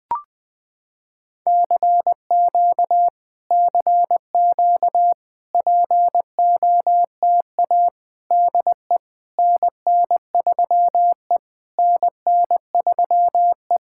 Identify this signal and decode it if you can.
Morse code, heard as a keyed tone, about 21 wpm, 700 Hz. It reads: CQ CQ POTA DE NN3E NN3E
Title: POTA QSO Practice - 20wpm